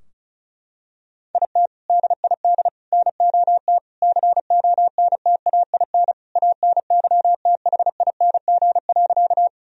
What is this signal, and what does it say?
Morse code, heard as a keyed tone, about 36 wpm, 700 Hz. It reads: IT DID NOT CONTAIN ANYTHING.
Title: Sentences from Top 1000 Words - Review Part 2 - 35wpm